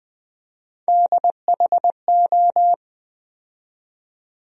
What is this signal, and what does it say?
Morse code, heard as a keyed tone, about 20 wpm, 700 Hz. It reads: DHO
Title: USA Call Sign Postfixes - 3 Letters 20wpm